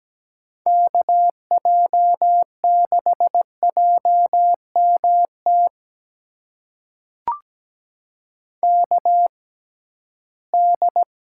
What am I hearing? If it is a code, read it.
Morse code, heard as a keyed tone, about 17 wpm, spaced as for 13 wpm, 700 Hz. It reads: KJ6JMT KD
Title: USA Call Signs - 6 Characters - 6x Character Spacing - 17wpm